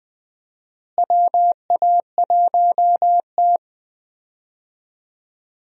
Morse code, heard as a keyed tone, about 20 wpm, 700 Hz. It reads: WA1T
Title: USA Call Signs - 4 Characters - Repeated 3x - 20wpm